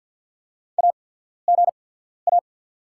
Morse code, heard as a keyed tone, about 48 wpm, spaced as for 14 wpm, 690 Hz. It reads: AGA
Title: 8x Character Spacing - Top 200 Words - 50wpm